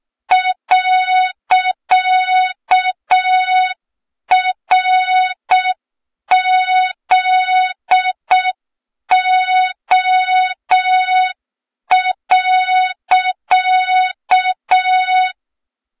Morse code, heard as a keyed tone, about 6 wpm, 3700 Hz. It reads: .RZO.
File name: Morze-HA2MN / hang / 23 GYAKORLAT